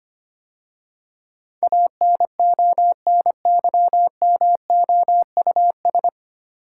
Morse code, heard as a keyed tone, about 25 wpm, 700 Hz. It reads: ANONYMOUS
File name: practice / 25wpm / 9 Letter Words - First 3 Letters Given - 25wpm